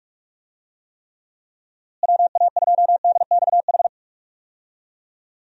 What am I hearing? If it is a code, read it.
Morse code, heard as a keyed tone, about 46 wpm, 700 Hz. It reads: WA2DXH
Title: USA Call Signs - 6 Characters - Repeated 2x - 45wpm